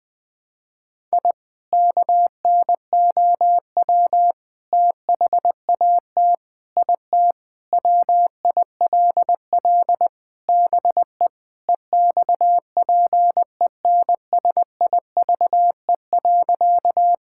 Morse code, heard as a keyed tone, about 21 wpm, 700 Hz. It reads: I KNOW THAT IT WILL BE EXPENSIVE.
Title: Sentences from Top 1000 Words - Review Part 6 - 20wpm